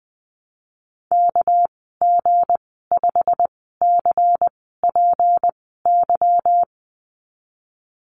Morse code, heard as a keyed tone, about 19 wpm, spaced as for 14 wpm, 700 Hz. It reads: KG5CPY
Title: 231 Lesson 37g - Letter x - Call Signs 20wpm